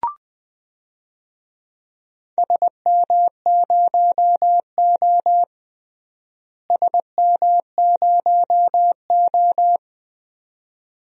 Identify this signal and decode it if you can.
Morse code, heard as a keyed tone, about 20 wpm, 700 Hz. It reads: SM0O SM0O
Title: DXCC-100 Call Signs - 4 Characters - Repeated 2x - 20wpm